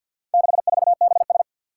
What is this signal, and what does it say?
Morse code, heard as a keyed tone, about 53 wpm, 700 Hz. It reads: 64BS
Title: Four Challenging Letter-Number Combinations 50wpm